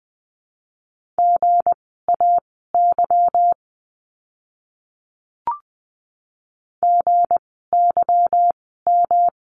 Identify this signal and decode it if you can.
Morse code, heard as a keyed tone, about 20 wpm, spaced as for 14 wpm, 700 Hz. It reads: GAY GYM
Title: 160 Lesson 26f - Letter g - Words 20wpm